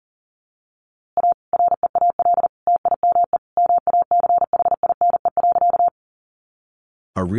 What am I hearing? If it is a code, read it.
Morse code, heard as a keyed tone, about 40 wpm, 700 Hz. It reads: A REAL TIME MACHINE.
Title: A Sound Of Thunder 40wpm